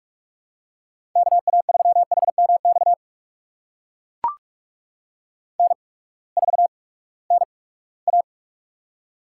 Morse code, heard as a keyed tone, about 44 wpm, 690 Hz. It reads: K A 3 H M X N 4 N A
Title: USA Call Signs - 8x Character Spacing - 45wpm